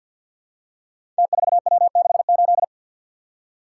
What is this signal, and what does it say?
Morse code, heard as a keyed tone, about 50 wpm, 700 Hz. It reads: T4W68
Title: Code Groups - 5 Letter:Numbers - 50wpm